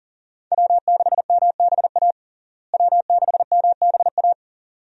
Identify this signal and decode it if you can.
Morse code, heard as a keyed tone, about 40 wpm, 700 Hz. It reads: W6MBA W6MBA
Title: USA Call Signs - 5 Characters - Repeated 2x - 40wpm